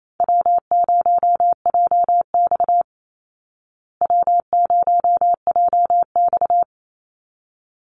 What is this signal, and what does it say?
Morse code, heard as a keyed tone, about 28 wpm, 700 Hz. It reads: W0JX W0JX
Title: Easy Call Signs Repeated 2x - 28wpm